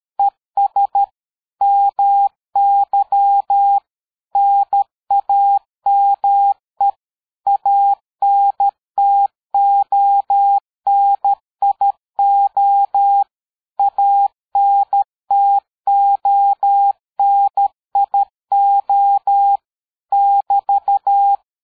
Morse code, heard as a keyed tone, about 13 wpm, 800 Hz. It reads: ES MY NAME ANTONIO ANTONIO <BT>
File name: dk5ke / QSO 60BpM